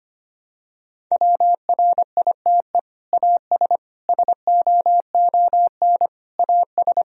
Morse code, heard as a keyed tone, about 25 wpm, 700 Hz. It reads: WRITE AS SOON AS
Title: Sentences from Top 1000 Words - Review Part 1 - 25wpm